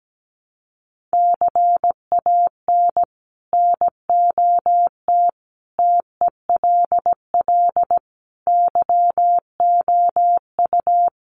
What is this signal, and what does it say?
Morse code, heard as a keyed tone, about 17 wpm, 700 Hz. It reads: CAN NOT TELL YOU
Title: Sets of 5 Words Encore2 17wpm